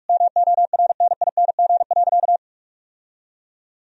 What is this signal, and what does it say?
Morse code, heard as a keyed tone, about 48 wpm, 700 Hz. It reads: MORNING.
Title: Sentences from Top 1000 Words - Review Part 3 - 45wpm